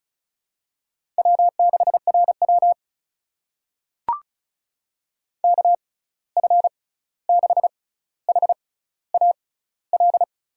Morse code, heard as a keyed tone, about 36 wpm, spaced as for 16 wpm, 700 Hz. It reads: W6RW KF6HAL
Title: USA Call Signs - 6x Character Spacing - 35wpm